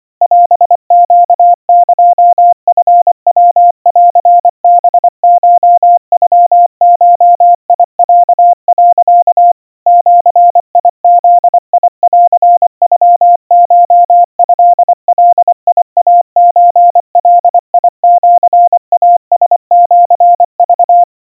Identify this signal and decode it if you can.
Morse code (Japanese Wabun code), heard as a keyed tone, about 25 wpm, 700 Hz. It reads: オネエチヤンハコノゴロ、 ジブンノコトガイソガシイラシク
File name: 90 cpm 700 Hz Crispy